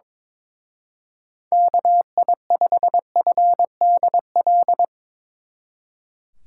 Morse code, heard as a keyed tone, about 22 wpm, 700 Hz. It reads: KI5FDL
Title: Speed Racing - USA 2x3 Call Signs 22wpm